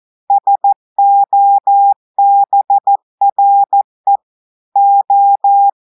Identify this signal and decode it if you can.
Morse code, heard as a keyed tone, about 14 wpm, 800 Hz. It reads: SOBRE O
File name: audios / portugais / portugais texte en clair 1 vitesse 14 mots minutes